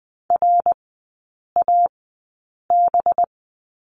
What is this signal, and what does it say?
Morse code, heard as a keyed tone, about 20 wpm, spaced as for 8 wpm, 700 Hz. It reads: RAB
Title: 198 Lesson 32e - Letter k - Three Characters 20wpm